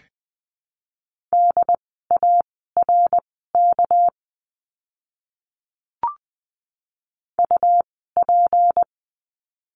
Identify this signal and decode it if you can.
Morse code, heard as a keyed tone, about 20 wpm, spaced as for 14 wpm, 700 Hz. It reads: DARK UP
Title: 199 Lesson 32f - Letter k - Words 20wpm